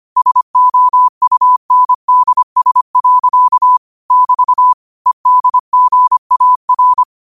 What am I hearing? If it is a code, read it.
Morse code, heard as a keyed tone, about 25 wpm, 1000 Hz. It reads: SOUNDS. <BT> EDGAR
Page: Quotes for Sat, 16 Aug 2025 in Morse Code at 25 words per minute.